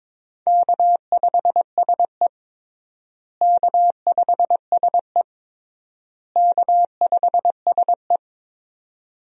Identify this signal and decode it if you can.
Morse code, heard as a keyed tone, about 22 wpm, 700 Hz. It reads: K5SE K5SE K5SE
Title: USA Call Signs - 4 Characters - Repeated 3x - 22wpm